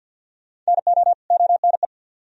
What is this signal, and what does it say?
Morse code, heard as a keyed tone, about 64 wpm, 700 Hz. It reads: NO ONE